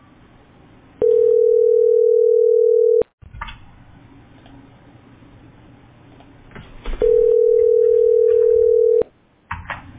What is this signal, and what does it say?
Ringback tone (North America), 440 + 480 Hz.